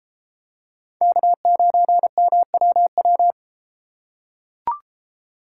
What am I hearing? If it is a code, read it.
Morse code, heard as a keyed tone, about 31 wpm, 700 Hz. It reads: K9MWW